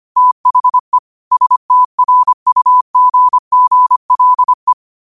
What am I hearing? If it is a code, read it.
Morse code, heard as a keyed tone, about 25 wpm, 1000 Hz. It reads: THE STRUGGLE
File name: qotdfiles / 25wpm